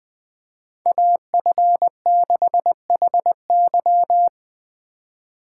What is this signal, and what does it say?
Morse code, heard as a keyed tone, about 24 wpm, 700 Hz. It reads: AF6HY
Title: Speed Racing - USA 2x2 Call Signs 20wpm